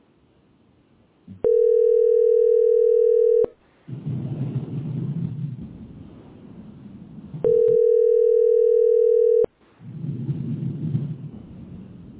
Ringback tone (North America), 440 + 480 Hz.